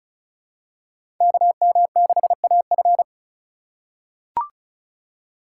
Morse code, heard as a keyed tone, about 34 wpm, 700 Hz. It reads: KM6AF
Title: USA Call Signs - 5 Characters - 3x Character Spacing - 35wpm